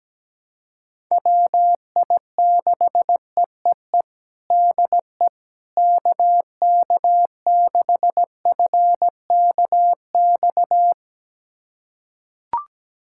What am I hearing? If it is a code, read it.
Morse code, heard as a keyed tone, about 17 wpm, 700 Hz. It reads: WI6EEE DE KK6FKX
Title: USA Callsign DE Callsign - 6 Characters - 17wpm